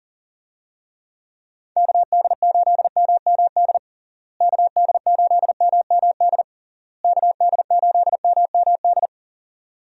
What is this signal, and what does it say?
Morse code, heard as a keyed tone, about 40 wpm, 700 Hz. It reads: KD8MMD KD8MMD KD8MMD
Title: USA Call Signs - 6 Characters - Repeated 3x - 40wpm